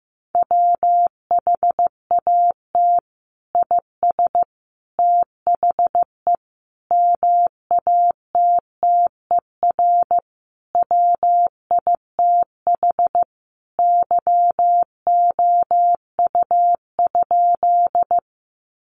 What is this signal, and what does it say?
Morse code, heard as a keyed tone, about 15 wpm, 700 Hz. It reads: WHAT IS THE MATTER WITH YOU?
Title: Sentences from Top 700 Words 15wpm